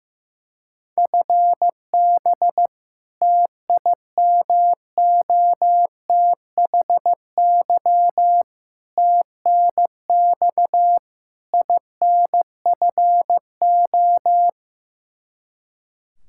Morse code, heard as a keyed tone, about 15 wpm, 700 Hz. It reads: FB TIMOTHY TNX INFO
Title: Practice QSO - Exchange 3 - More Info - 15wpm